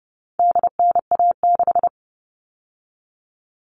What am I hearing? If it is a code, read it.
Morse code, heard as a keyed tone, about 30 wpm, 700 Hz. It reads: DNA6
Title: All Cut Numbers - 4-Digit 30wpm